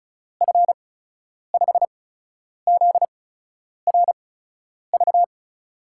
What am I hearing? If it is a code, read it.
Morse code, heard as a keyed tone, about 36 wpm, spaced as for 10 wpm, 700 Hz. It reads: F5ZRV
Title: USA Call Signs - 6 Characters - 8x Character Spacing - 35wpm